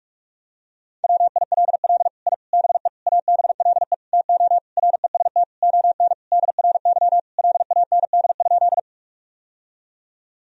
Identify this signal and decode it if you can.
Morse code, heard as a keyed tone, about 46 wpm, 700 Hz. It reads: WILL I BE ABLE TO REST ON DRY LAND?
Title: Sentences from Top 1000 Words - Review Part 3 - 45wpm